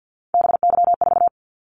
Morse code, heard as a keyed tone, about 50 wpm, 700 Hz. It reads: BY4
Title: Callsign Prefixes for Top 100 DXCC - 50wpm